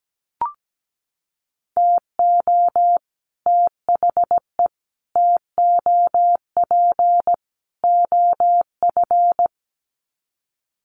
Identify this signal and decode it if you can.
Morse code, heard as a keyed tone, about 17 wpm, 700 Hz. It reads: TO THE TOP OF
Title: Sets of 4 Words 17wpm